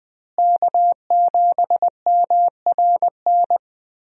Morse code, heard as a keyed tone, about 20 wpm, 700 Hz. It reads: K7MRN
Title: USA Call Signs - 5 Characters - Repeated 3x - 20wpm